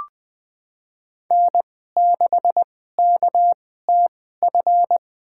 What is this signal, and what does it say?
Morse code, heard as a keyed tone, about 20 wpm, spaced as for 14 wpm, 700 Hz. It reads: N6KTF
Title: USA Call Signs - 5 Characters - 2x Character Spacing - 20wpm